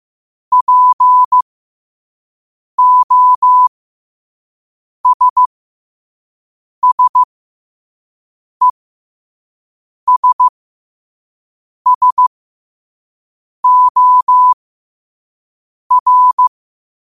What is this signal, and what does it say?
Morse code, heard as a keyed tone, about 15 wpm, spaced as for 5 wpm, 1000 Hz. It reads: POSSESSOR
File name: qotdfiles / 5wpm